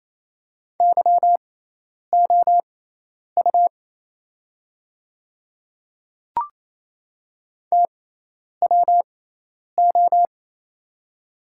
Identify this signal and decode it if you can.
Morse code, heard as a keyed tone, about 28 wpm, spaced as for 10 wpm, 690 Hz. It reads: YOU TWO
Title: 6x Character Spacing - Top 100 Words - 28wpm